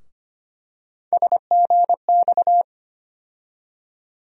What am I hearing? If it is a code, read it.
Morse code, heard as a keyed tone, about 25 wpm, 700 Hz. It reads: SGX
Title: USA Call Sign Postfixes - 3 Letters 25wpm